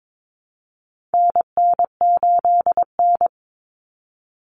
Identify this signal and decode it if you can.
Morse code, heard as a keyed tone, about 22 wpm, 700 Hz. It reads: NN8N
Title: Common Cut Numbers - 4-Digit 22wpm